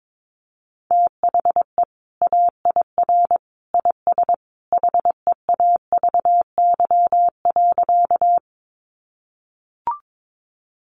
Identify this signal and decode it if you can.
Morse code, heard as a keyed tone, about 22 wpm, 700 Hz. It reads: THE AIR IS HEAVY.
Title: Sentences from Top 1100 Words 22wpm